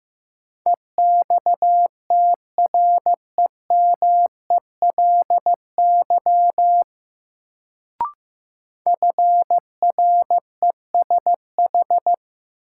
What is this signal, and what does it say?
Morse code, heard as a keyed tone, about 15 wpm, 700 Hz. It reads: EXTREMELY FRESH
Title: Sentences from Top 1200 Words - New Word Drill Alt - 15wpm